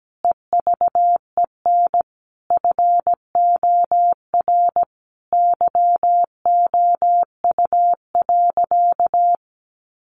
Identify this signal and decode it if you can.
Morse code, heard as a keyed tone, about 17 wpm, 700 Hz. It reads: EVEN FOR YOU.
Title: Sentences from Top 700 Words 17wpm